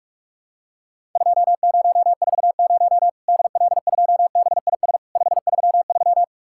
Morse code, heard as a keyed tone, about 44 wpm, 700 Hz. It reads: 2040 DL2BIS 533
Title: Worked All Europe DX Contest - QTC Exchange - 45wpm